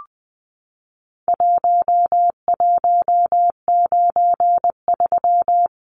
Morse code, heard as a keyed tone, about 20 wpm, 700 Hz. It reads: 1193
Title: Numbers - Contest Style - 4-Digit 20wpm